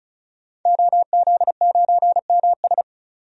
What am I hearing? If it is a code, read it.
Morse code, heard as a keyed tone, about 36 wpm, 700 Hz. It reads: OZ9MS
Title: DXCC-100 Call Signs - 5 Characters - 35wpm